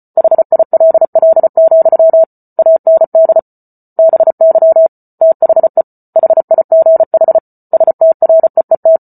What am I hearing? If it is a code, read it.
Morse code, heard as a keyed tone, about 34 wpm, 650 Hz. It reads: HILL, AND BY THE HIGH STREET